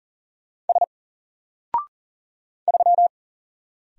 Morse code, heard as a keyed tone, about 40 wpm, 700 Hz. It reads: S 3
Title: Single Letter-Number Alt 40wpm